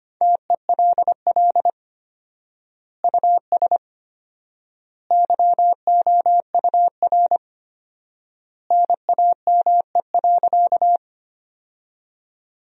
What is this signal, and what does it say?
Morse code, heard as a keyed tone, about 25 wpm, 700 Hz. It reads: TELL US YOUR NAME.
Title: 4x Word-Spacing - Sentences from Top 200 Words - 25wpm